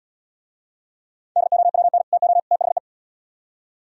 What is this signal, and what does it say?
Morse code, heard as a keyed tone, about 77 wpm, 700 Hz. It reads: DOWN TO THE